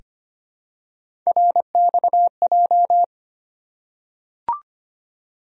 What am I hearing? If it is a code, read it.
Morse code, heard as a keyed tone, about 25 wpm, 700 Hz. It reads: RXJ
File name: Three Letter-Number Combinations - 25wpm